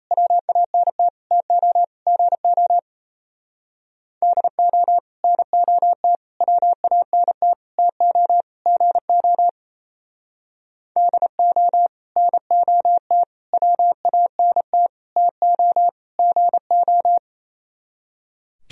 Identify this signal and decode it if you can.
Morse code, heard as a keyed tone, about 35 wpm, 700 Hz. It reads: WANT TO GO DO NOT WANT TO GO DO NOT WANT TO GO
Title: Speed-Racing - Sets of 5 Words 28wpm